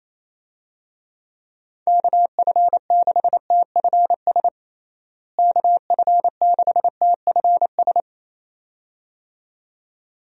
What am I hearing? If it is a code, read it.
Morse code, heard as a keyed tone, about 28 wpm, 700 Hz. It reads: KF6TFS KF6TFS
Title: USA Call Signs - 6 Characters - Repeated 2x - 28wpm